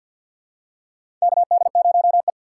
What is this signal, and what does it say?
Morse code, heard as a keyed tone, about 48 wpm, 700 Hz. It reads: KD0E
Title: USA Call Signs - 3x Character Spacing - 50wpm